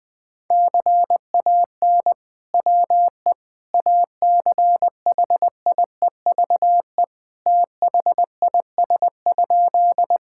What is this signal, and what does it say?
Morse code, heard as a keyed tone, about 19 wpm, 700 Hz. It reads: CAN WE ACHIEVE THIS?
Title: Sentences from Top 1000 Words - Review Part 6 - 20wpm